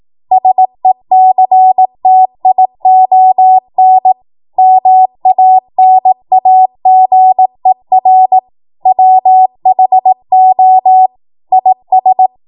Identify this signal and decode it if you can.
Morse code, heard as a keyed tone, about 18 wpm, 750 Hz. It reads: SECTION MANAGER WHO IS
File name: Morse / 18WPM